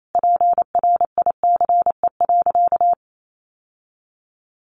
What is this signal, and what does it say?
Morse code, heard as a keyed tone, about 28 wpm, 700 Hz. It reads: PRICE.
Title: Sentences from Top 1200 Words 28wpm